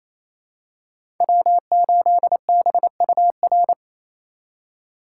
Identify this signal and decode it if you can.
Morse code, heard as a keyed tone, about 27 wpm, 700 Hz. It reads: W8BUR
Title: USA Call Signs - 6x Character Spacing - 28wpm